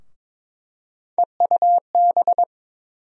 Morse code, heard as a keyed tone, about 21 wpm, 700 Hz. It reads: EUB
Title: USA Call Sign Postfixes - 3 Letters 22wpm